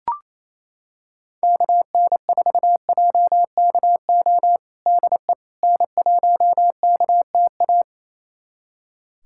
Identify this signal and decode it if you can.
Morse code, heard as a keyed tone, about 28 wpm, 700 Hz. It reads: KN4JKO DE N1KTA
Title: USA Callsign DE Callsign 28wpm